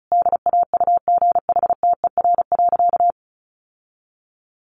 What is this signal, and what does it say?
Morse code, heard as a keyed tone, about 34 wpm, 700 Hz. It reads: DAUGHTER.
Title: Sentences from Top 1500 Words 35wpm